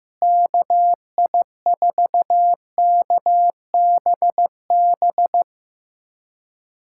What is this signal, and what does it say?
Morse code, heard as a keyed tone, about 15 wpm, 700 Hz. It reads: KI4KBB
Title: USA Callsign DE Callsign - 6 Characters - 15wpm